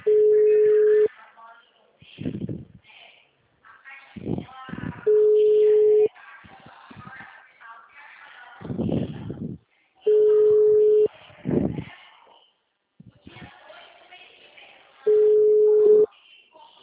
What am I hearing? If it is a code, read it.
Ringback tone (Europe, 425 Hz), with speech.
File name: out- -2004- - - .15